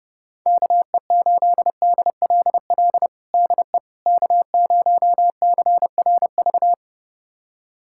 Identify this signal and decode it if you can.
Morse code, heard as a keyed tone, about 32 wpm, 700 Hz. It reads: KE8DLL DE K0CRV
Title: USA Callsign DE Callsign 30wpm